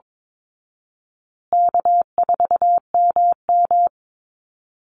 Morse code, heard as a keyed tone, about 22 wpm, 700 Hz. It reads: K4MM
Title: Call Signs Repeated 3x - 22wpm